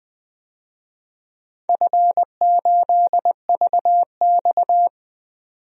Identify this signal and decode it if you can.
Morse code, heard as a keyed tone, about 20 wpm, 700 Hz. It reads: F8VX
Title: DXCC-100 Call Signs - 4 Characters - 20wpm